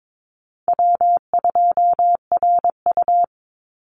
Morse code, heard as a keyed tone, about 22 wpm, 690 Hz. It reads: W2RU
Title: Easy Call Signs Repeated 2x - 22wpm